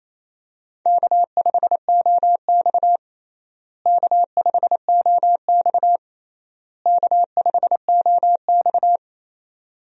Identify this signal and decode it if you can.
Morse code, heard as a keyed tone, about 28 wpm, 700 Hz. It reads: K5OX K5OX K5OX
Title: USA Call Signs - 4 Characters - Repeated 3x - 28wpm